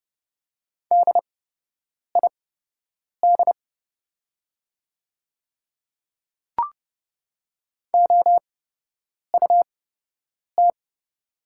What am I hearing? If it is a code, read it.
Morse code, heard as a keyed tone, about 30 wpm, spaced as for 8 wpm, 700 Hz. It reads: DID OUT
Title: 8x Character Spacing - Top 200 Words - 30wpm